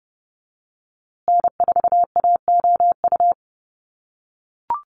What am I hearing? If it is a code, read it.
Morse code, heard as a keyed tone, about 30 wpm, 700 Hz. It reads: N4AOU
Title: USA Call Signs Repeated 2x 30wpm